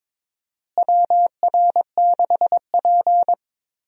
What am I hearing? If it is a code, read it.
Morse code, heard as a keyed tone, about 22 wpm, 700 Hz. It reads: WR6P
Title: USA Call Signs - 4 Characters - Repeated 2x - 22wpm